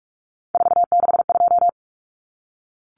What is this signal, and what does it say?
Morse code, heard as a keyed tone, about 44 wpm, 700 Hz. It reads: 462
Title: Common Cut Numbers - 3-Digit 45wpm